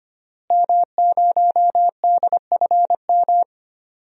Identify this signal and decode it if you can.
Morse code, heard as a keyed tone, about 25 wpm, 700 Hz. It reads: M0DFM
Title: DXCC-100 Call Signs - 5 Characters - Repeated 3x - 25wpm